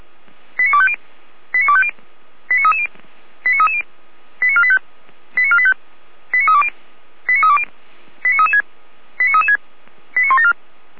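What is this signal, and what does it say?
Five-tone selective call (ZVEI), heard as quick runs of five tones, tones of about 70 ms. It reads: 89250 89250 89330 89330 89485 89485 89320 89320 89336 89336 89174
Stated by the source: ZVEI five-tone call 89319 received by SDR, archived by Feuerwehr Orlamünde (fire brigade).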